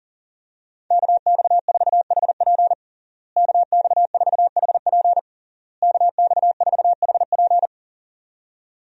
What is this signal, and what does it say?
Morse code, heard as a keyed tone, about 40 wpm, 700 Hz. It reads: KX4HP KX4HP KX4HP
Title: USA Call Signs - 5 Characters - Repeated 3x - 40wpm